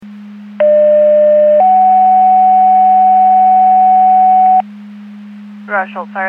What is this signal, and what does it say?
Two-tone page, heard as a one-second tone, then a three-second tone, about 600 then 750 Hz, with speech. It is heard from 0.6 to 4.6 s.